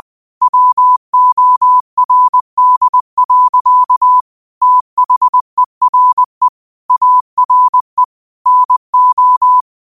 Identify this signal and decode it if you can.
Morse code, heard as a keyed tone, about 20 wpm, 1000 Hz. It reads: WORD. THERE ARE NO
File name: qotdfiles / 20wpm